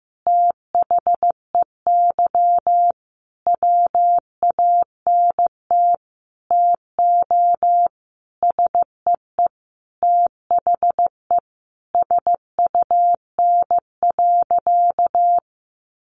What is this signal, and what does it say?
Morse code, heard as a keyed tone, about 15 wpm, 700 Hz. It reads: THEY WANT TO SEE THE SUN.
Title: Easy Phrases 15wpm